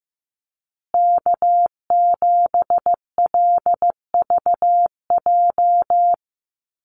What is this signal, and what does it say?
Morse code, heard as a keyed tone, about 15 wpm, 700 Hz. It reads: K7LVJ
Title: Call Signs Repeated 3x - 15wpm